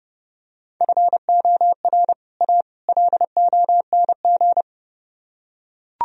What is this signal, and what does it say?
Morse code, heard as a keyed tone, about 32 wpm, 700 Hz. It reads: FOR A LONG